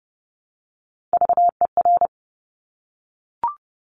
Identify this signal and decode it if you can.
Morse code, heard as a keyed tone, about 30 wpm, 700 Hz. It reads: VER
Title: Top Three Letter Combinations in Words 30wpm